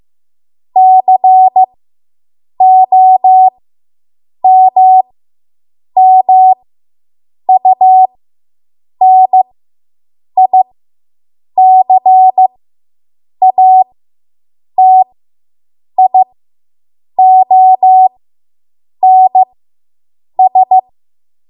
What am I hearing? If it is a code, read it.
Morse code, heard as a keyed tone, about 15 wpm, spaced as for 7 wpm, 750 Hz. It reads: COMMUNICATIONS